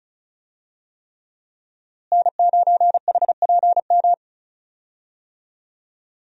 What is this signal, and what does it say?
Morse code, heard as a keyed tone, about 34 wpm, 700 Hz. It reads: N9HPM
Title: USA Call Signs - 5 Characters - Repeated 3x - 35wpm